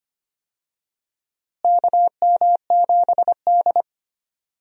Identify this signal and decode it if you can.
Morse code, heard as a keyed tone, about 25 wpm, 700 Hz. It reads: KM7D
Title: USA Call Signs - 4 Characters - Repeated 2x - 25wpm